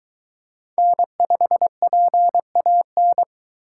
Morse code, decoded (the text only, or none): N5PAN